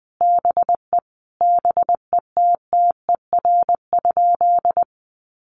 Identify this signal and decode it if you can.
Morse code, heard as a keyed tone, about 20 wpm, 700 Hz. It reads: BE BETTER?
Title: Sentences from Top 400 Words 20wpm